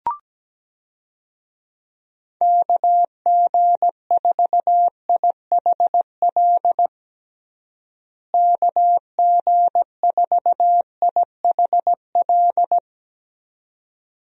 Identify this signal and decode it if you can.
Morse code, heard as a keyed tone, about 17 wpm, 700 Hz. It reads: KG4IHL KG4IHL
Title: USA Call Signs - 6 Characters - Repeated 2x - 17wpm